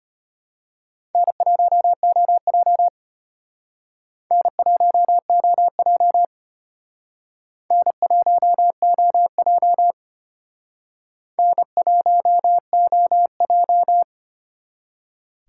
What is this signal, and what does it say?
Morse code, heard as a keyed tone, about 32 wpm, 700 Hz. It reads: N1OJ N1OJ N1OJ N1OJ